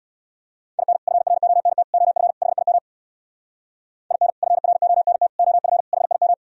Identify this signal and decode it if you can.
Morse code, heard as a keyed tone, about 64 wpm, 700 Hz. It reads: IN FRONT OF HIM IN FRONT OF HIM